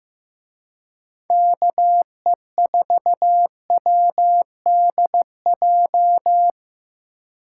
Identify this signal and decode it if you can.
Morse code, heard as a keyed tone, about 17 wpm, 700 Hz. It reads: KE4WDJ
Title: Speed Racing - USA Call Signs 15wpm